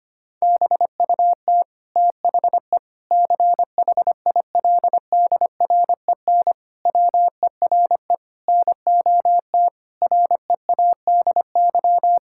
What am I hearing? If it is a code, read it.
Morse code, heard as a keyed tone, about 25 wpm, 700 Hz. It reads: BUT THE CHILDREN WERE NOT READY
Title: Sentences from Top 1000 Words - Review Part 6 - 25wpm